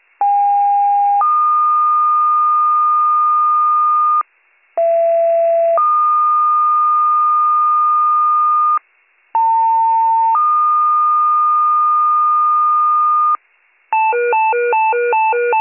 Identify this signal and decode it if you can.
Two-tone page, heard as a one-second tone, then a three-second tone, about 810 then 1220 Hz, about 670 then 1220 Hz, about 900 then 1220 Hz. It is heard from 0.2 to 13.4 s.